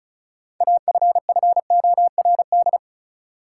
Morse code, heard as a keyed tone, about 36 wpm, 700 Hz. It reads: AFFORD
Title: Sentences from Top 1100 Words - New Word Drill Alt - 35wpm